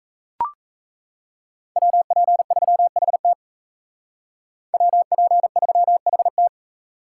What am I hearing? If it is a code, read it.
Morse code, heard as a keyed tone, about 33 wpm, 700 Hz. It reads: WP3HT WP3HT
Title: Speed Racing - USA 2x2 Call Signs 28wpm